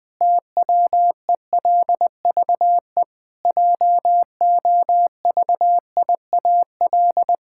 Morse code, heard as a keyed tone, about 20 wpm, 700 Hz. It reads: TWELVE JOVIAL
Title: Pangram Challenge v2 - 20wpm